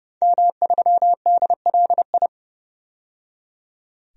Morse code, heard as a keyed tone, about 29 wpm, 700 Hz. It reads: M3DLI
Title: DXCC-100 Call Signs - 5 Characters - 30wpm